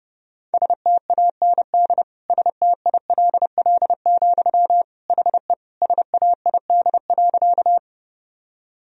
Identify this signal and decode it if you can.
Morse code, heard as a keyed tone, about 30 wpm, 700 Hz. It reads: STAND STILL, HE SAID.
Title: Sentences from Top 1000 Words - Review Part 1 - 30wpm